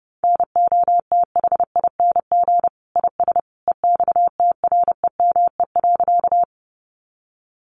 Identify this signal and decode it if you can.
Morse code, heard as a keyed tone, about 30 wpm, 700 Hz. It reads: NOTHING IS EXTREME.
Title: Sentences from Top 1300 Words 30wpm